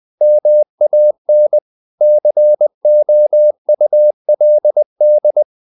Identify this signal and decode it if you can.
Morse code, heard as a keyed tone, about 20 wpm, 600 Hz. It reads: MAN COULD
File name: the old man and the sea part 2 20wpm